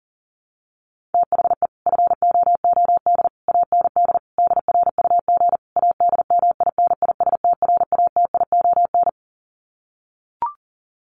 Morse code, heard as a keyed tone, about 40 wpm, 700 Hz. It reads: THE FOOD AND DRUG ADMINISTRATION
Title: Sets of 5 Words 40wpm